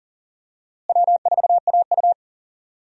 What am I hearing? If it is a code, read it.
Morse code, heard as a keyed tone, about 37 wpm, 700 Hz. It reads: W4AU
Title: USA Call Signs - 4 Characters - Repeated 2x - 40wpm